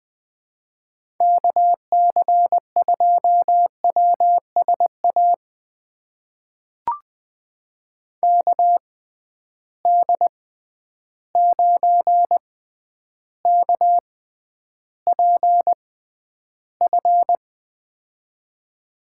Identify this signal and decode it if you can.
Morse code, heard as a keyed tone, about 21 wpm, spaced as for 10 wpm, 700 Hz. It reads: KC2WSA KD9KPF
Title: USA Call Signs - 6 Characters - 6x Character Spacing - 20wpm